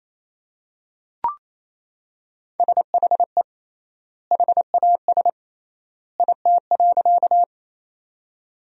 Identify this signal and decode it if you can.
Morse code, heard as a keyed tone, about 28 wpm, 700 Hz. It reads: SHE HAS IT.
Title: 3x Word-Spacing - Sentences from Top 100 Words - 28wpm